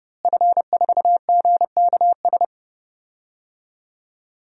Morse code, heard as a keyed tone, about 31 wpm, 700 Hz. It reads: F4GKS
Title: DXCC-100 Call Signs - 5 Characters - Repeated 2x - 30wpm